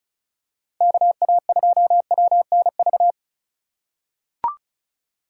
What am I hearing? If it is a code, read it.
Morse code, heard as a keyed tone, about 34 wpm, 700 Hz. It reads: KA2WNV
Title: USA Call Signs - 6 Characters - 4x Character Spacing - 35wpm